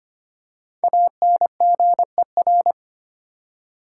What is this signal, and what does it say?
Morse code, heard as a keyed tone, about 25 wpm, 700 Hz. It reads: ANGER
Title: Five Letter Words - 25wpm